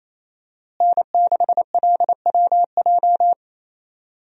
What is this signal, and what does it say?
Morse code, heard as a keyed tone, about 28 wpm, 700 Hz. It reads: N6LWJ
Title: USA Call Signs - 5 Characters - Repeated 3x - 28wpm